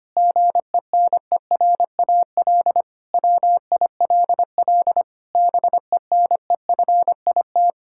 Morse code, heard as a keyed tone, about 25 wpm, 700 Hz. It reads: GENERAL WILL BENEFIT
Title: Sentences from Top 1000 Words - Review Part 6 - 25wpm